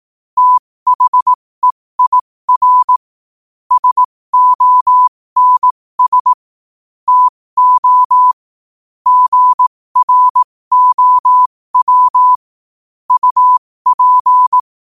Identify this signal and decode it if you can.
Morse code, heard as a keyed tone, about 18 wpm, spaced as for 15 wpm, 1000 Hz. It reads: THEIR SONS TO GROW UP
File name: qotdfiles / 15wpm